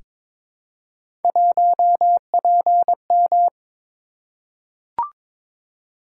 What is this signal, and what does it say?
Morse code, heard as a keyed tone, about 22 wpm, 700 Hz. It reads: 1PM
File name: practice / 22wpm / Three Letter-Number Combinations - 22wpm